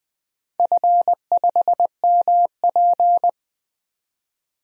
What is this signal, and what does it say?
Morse code, heard as a keyed tone, about 20 wpm, 700 Hz. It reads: F5MP
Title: DXCC-100 Call Signs - 4 Characters - 20wpm